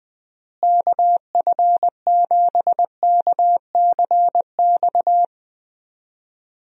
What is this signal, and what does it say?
Morse code, heard as a keyed tone, about 21 wpm, 700 Hz. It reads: KF7KCX
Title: USA Call Signs - 6 Characters - Repeated 3x - 20wpm